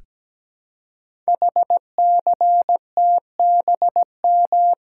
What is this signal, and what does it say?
Morse code, heard as a keyed tone, about 17 wpm, 700 Hz. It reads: HCTBM
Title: Code Groups - 5 Letters - 17wpm